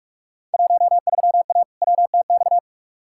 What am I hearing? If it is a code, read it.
Morse code, heard as a keyed tone, about 43 wpm, 700 Hz. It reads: 13A WTX
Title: Field Day Exchanges Encore - 45wpm